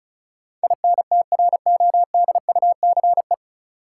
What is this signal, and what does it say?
Morse code, heard as a keyed tone, about 36 wpm, 700 Hz. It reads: INTRODUCE
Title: Sentences from Top 1600 Words - New Word Drill - 35wpm